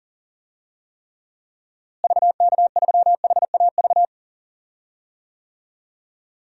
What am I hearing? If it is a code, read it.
Morse code, heard as a keyed tone, about 37 wpm, 700 Hz. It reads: VK3HAV
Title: DXCC-100 Call Signs - 6 Characters - Repeated 2x - 40wpm